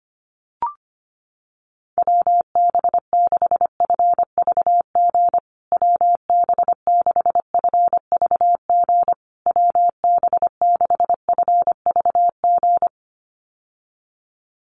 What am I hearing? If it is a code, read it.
Morse code, heard as a keyed tone, about 25 wpm, 700 Hz. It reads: WB6FVG WB6FVG WB6FVG
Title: USA Call Signs Repeated 3x 25wpm